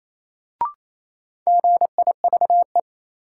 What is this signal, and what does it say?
Morse code, heard as a keyed tone, about 28 wpm, 700 Hz. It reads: GIVE
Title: Sentences from Top 200 Words - New Word Drill - 28wpm